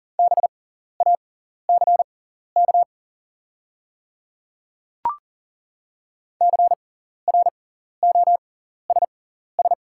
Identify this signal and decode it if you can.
Morse code, heard as a keyed tone, about 40 wpm, spaced as for 14 wpm, 700 Hz. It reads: BACK CROSS
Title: 6x Character Spacing - Top 300 Words - 40wpm